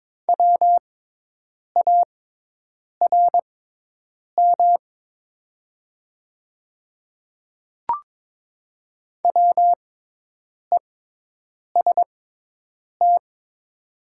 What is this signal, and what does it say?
Morse code, heard as a keyed tone, about 22 wpm, spaced as for 8 wpm, 700 Hz. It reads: WARM WEST
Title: 6x Character Spacing - Top 500 Words - 22wpm